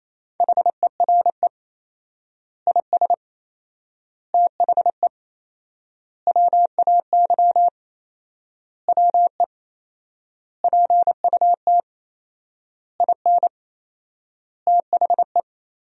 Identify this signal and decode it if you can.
Morse code, heard as a keyed tone, about 28 wpm, 700 Hz. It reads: HERE IS THE WAY WE PUT IN THE
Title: 4x Word-Spacing - Sentences from Top 200 Words - 28wpm